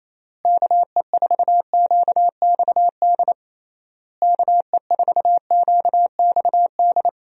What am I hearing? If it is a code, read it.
Morse code, heard as a keyed tone, about 28 wpm, 700 Hz. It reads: KE4QXD KE4QXD
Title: USA Call Signs - 6 Characters - Repeated 3x - 28wpm